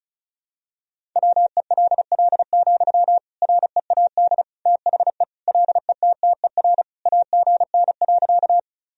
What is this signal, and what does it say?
Morse code, heard as a keyed tone, about 36 wpm, 700 Hz. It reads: WELL, READ THE LETTER AGN.
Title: Sentences from Top 300 Words and Abbreviations 35wpm